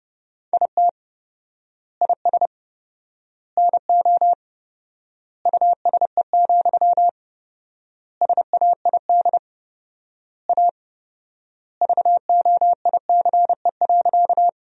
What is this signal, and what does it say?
Morse code, heard as a keyed tone, about 30 wpm, 700 Hz. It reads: IT IS NO USE, SAID A VOICE.
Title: 4x Word-Spacing - Sentences from Top 500 Words - 30wpm